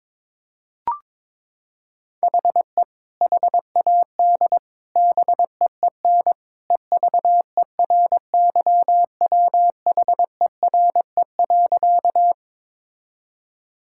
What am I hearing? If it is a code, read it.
Morse code, heard as a keyed tone, about 22 wpm, 700 Hz. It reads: HE HAD BEEN EVERYWHERE.
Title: Sentences from Top 1600 Words 22wpm